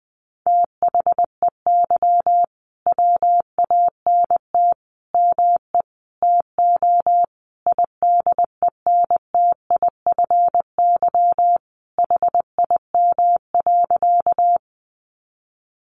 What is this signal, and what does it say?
Morse code, heard as a keyed tone, about 20 wpm, 700 Hz. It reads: THEY WANT ME TO IDENTIFY HIM.
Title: Sentences from Top 900 Words 20wpm